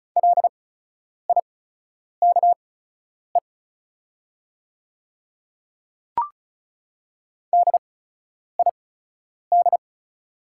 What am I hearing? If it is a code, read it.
Morse code, heard as a keyed tone, about 36 wpm, spaced as for 10 wpm, 700 Hz. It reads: LIKE DID
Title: 8x Character Spacing - Top 100 Words - 35wpm